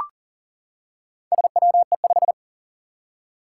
Morse code, heard as a keyed tone, about 40 wpm, 700 Hz. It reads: SWE5
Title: Four Challenging Letter-Number Combinations 40wpm